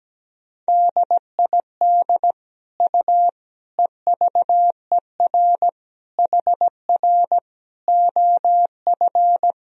Morse code, heard as a keyed tone, about 17 wpm, 700 Hz. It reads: DID U EVER HR OF
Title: Sentences from Top 400 Words and Abbreviations 17wpm